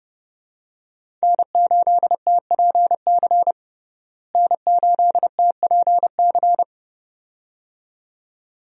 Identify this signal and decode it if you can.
Morse code, heard as a keyed tone, about 30 wpm, 700 Hz. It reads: N8TPC N8TPC
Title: USA Call Signs - 5 Characters - Repeated 2x - 30wpm